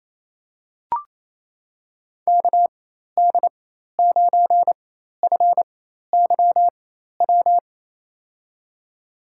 Morse code, heard as a keyed tone, about 28 wpm, spaced as for 13 wpm, 700 Hz. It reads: KD9FYW
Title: USA Call Signs - 6 Characters - 4x Character Spacing - 28wpm